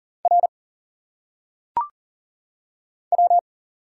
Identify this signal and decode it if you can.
Morse code, heard as a keyed tone, about 40 wpm, 700 Hz. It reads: R W
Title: Single Letters - Phonetic Pronunciation 40wpm